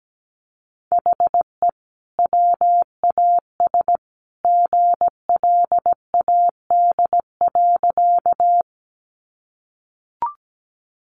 Morse code, heard as a keyed tone, about 17 wpm, 700 Hz. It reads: HE WAS GLAD.
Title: Sentences from Top 800 Words 17wpm